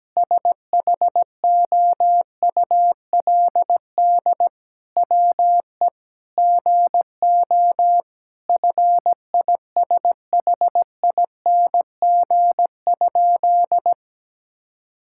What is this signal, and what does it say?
Morse code, heard as a keyed tone, about 17 wpm, 700 Hz. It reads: SHOULD WE GO FISHING?
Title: Sentences from Top 1000 Words - Review Part 5 - 17wpm